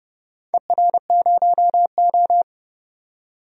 Morse code, heard as a keyed tone, about 29 wpm, 700 Hz. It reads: ER0O
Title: DXCC-100 Call Signs - 4 Characters - 30wpm